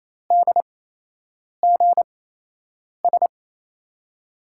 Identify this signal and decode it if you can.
Morse code, heard as a keyed tone, about 28 wpm, spaced as for 8 wpm, 700 Hz. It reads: DGS
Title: USA Call Signs - 8x Character Spacing - 28wpm